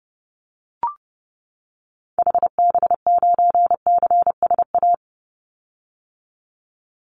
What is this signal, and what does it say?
Morse code, heard as a keyed tone, about 30 wpm, 700 Hz. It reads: HB9CSA
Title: Call Signs - 30wpm